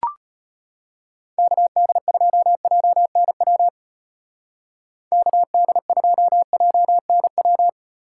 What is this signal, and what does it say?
Morse code, heard as a keyed tone, about 32 wpm, 700 Hz. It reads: KD2JNW KD2JNW